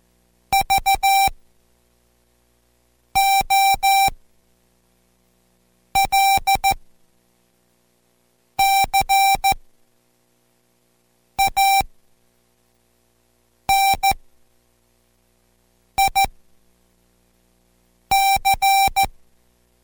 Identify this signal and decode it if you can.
Morse code, heard as a keyed tone, about 14 wpm, spaced as for 4 wpm, 780 Hz. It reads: VOLCANIC